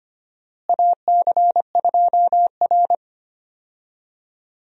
Morse code, heard as a keyed tone, about 25 wpm, 700 Hz. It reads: AC2R
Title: USA Call Signs - 4 Characters - Repeated 2x - 25wpm